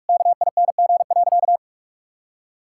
Morse code, heard as a keyed tone, about 46 wpm, 700 Hz. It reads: KING.
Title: Sentences from Top 1000 Words - Review Part 6 - 45wpm